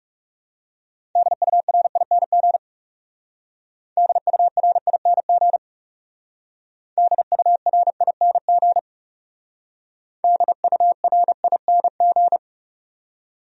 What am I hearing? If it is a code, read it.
Morse code, heard as a keyed tone, about 36 wpm, 700 Hz. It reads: DURING DURING DURING DURING
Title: Speed-Racing Top 500 Words 30wpm